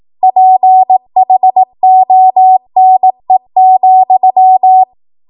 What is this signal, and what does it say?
Morse code, heard as a keyed tone, about 18 wpm, 750 Hz. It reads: PHONE,
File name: Archive / 18 WPM / 18WPM